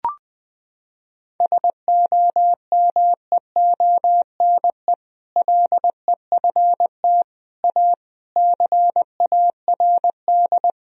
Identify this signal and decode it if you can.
Morse code, heard as a keyed tone, about 20 wpm, 700 Hz. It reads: SOMEONE LEFT A CARD
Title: Sentences from Top 1000 Words - Review Part 4 - 20wpm